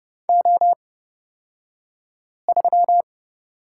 Morse code, heard as a keyed tone, about 30 wpm, spaced as for 5 wpm, 700 Hz. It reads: O3
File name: Lesson 38d - Letter q - Two Characters 30wpm